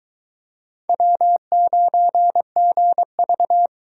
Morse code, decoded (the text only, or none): W9GV